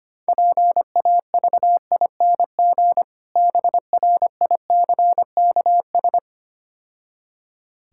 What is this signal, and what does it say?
Morse code, heard as a keyed tone, about 25 wpm, 690 Hz. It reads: PAVING BRICKS
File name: practice / 25wpm / Difficult Words - Speed-Racing with Hints - 25wpm